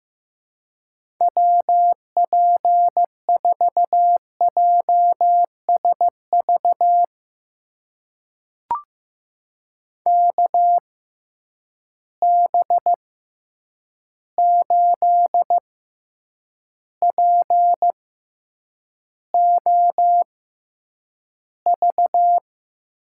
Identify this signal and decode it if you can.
Morse code, heard as a keyed tone, about 15 wpm, spaced as for 8 wpm, 700 Hz. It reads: WP4JSV KB8POV
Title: USA Call Signs - 6 Characters - 6x Character Spacing - 15wpm